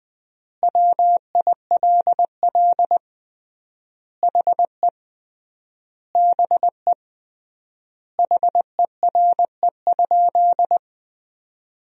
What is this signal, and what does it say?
Morse code, heard as a keyed tone, about 20 wpm, 700 Hz. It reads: WILL HE BE HERE?
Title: 3x Word-Spacing - Sentences from Top 400 Words - 20wpm